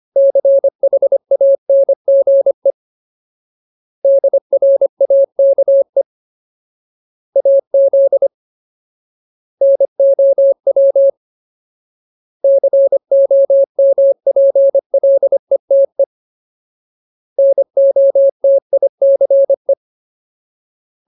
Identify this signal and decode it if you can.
Morse code, heard as a keyed tone, about 25 wpm, 550 Hz. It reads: CHANGE DRAKE AZ NOW COMPLETE NOTICE
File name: Ultimate Word List / 25WPM Ultimate